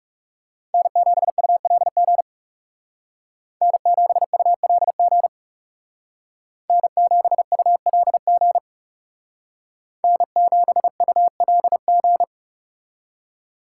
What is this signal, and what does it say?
Morse code, heard as a keyed tone, about 35 wpm, 700 Hz. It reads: N7ULG N7ULG N7ULG N7ULG
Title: Speed Racing - USA Call Signs 30wpm